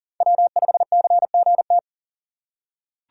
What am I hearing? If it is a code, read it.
Morse code, heard as a keyed tone, about 40 wpm, 700 Hz. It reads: W5CGT
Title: USA Call Signs - 5 Characters - 2x Character Spacing - 40wpm